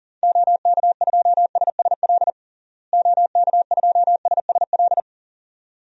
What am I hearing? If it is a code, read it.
Morse code, heard as a keyed tone, about 37 wpm, 700 Hz. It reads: OK2SSL OK2SSL
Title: DXCC-100 Call Signs - 6 Characters - Repeated 2x - 40wpm